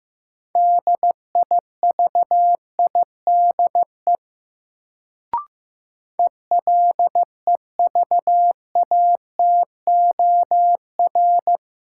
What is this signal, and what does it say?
Morse code, heard as a keyed tone, about 15 wpm, 700 Hz. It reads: DIVIDE ELEVATOR
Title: Sentences from Top 1600 Words - New Word Drill Alt - 15wpm